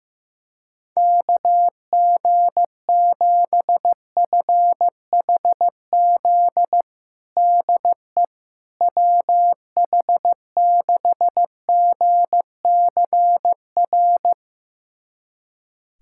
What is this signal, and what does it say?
Morse code, heard as a keyed tone, about 15 wpm, 700 Hz. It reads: KG7FHZ DE WH6GCR
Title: USA Callsign DE Callsign - 6 Characters - 15wpm